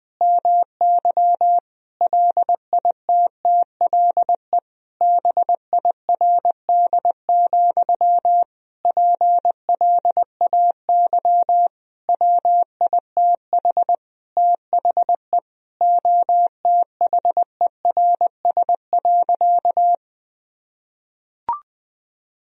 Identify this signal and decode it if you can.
Morse code, heard as a keyed tone, about 20 wpm, 700 Hz. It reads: MY LITTLE BIRD, PLAY WITH THE OTHERS.
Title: Sentences from Top 1000 Words - Review Part 2 - 20wpm